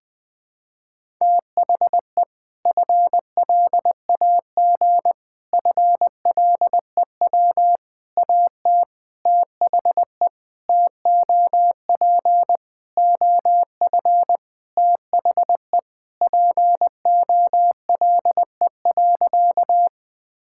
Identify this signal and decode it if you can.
Morse code, heard as a keyed tone, about 20 wpm, 700 Hz. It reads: THE FLAG FLEW AT THE TOP OF THE POLE.
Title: Sentences with 4-Word Phrases - 20wpm